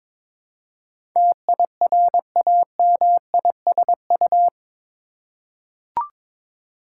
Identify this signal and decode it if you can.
Morse code, heard as a keyed tone, about 23 wpm, 690 Hz. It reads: TIRAMISU
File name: Difficult Words with Hints - 22wpm